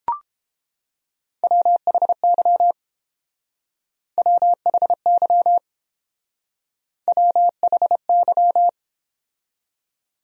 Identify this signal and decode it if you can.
Morse code, heard as a keyed tone, about 30 wpm, 700 Hz. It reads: WHY WHY WHY